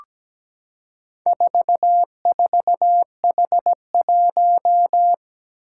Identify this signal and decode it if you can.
Morse code, heard as a keyed tone, about 17 wpm, 700 Hz. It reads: 44H1
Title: Four Challenging Letter-Number Combinations 17wpm